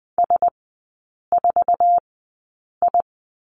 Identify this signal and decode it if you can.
Morse code, heard as a keyed tone, about 20 wpm, spaced as for 8 wpm, 700 Hz. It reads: S4I
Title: 047 Lesson 9e - Number 4 - Three Characters 20wpm